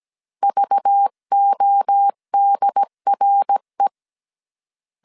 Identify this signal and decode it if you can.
Morse code, heard as a keyed tone, about 17 wpm, 800 Hz. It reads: VODRE
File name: CW / DET / 1 6 1500C S C 16wpm 2733